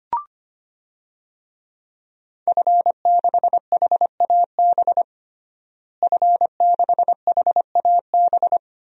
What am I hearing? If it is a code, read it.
Morse code, heard as a keyed tone, about 25 wpm, 700 Hz. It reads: F6HAB F6HAB
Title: DXCC-100 Call Signs - 5 Characters - Repeated 3x - 25wpm